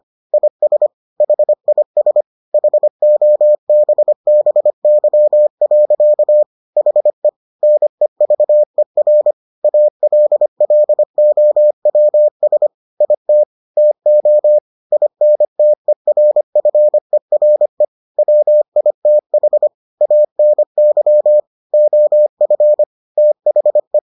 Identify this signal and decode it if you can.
Morse code, heard as a keyed tone, about 25 wpm, 600 Hz. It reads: IS HIS HOBBY. HE NEVER ALLOWS IT TO INTERFERE WITH ANY OF THE